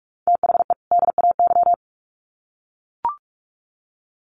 Morse code, heard as a keyed tone, about 44 wpm, 700 Hz. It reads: THE DAY
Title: Sets of 2 Words 45wpm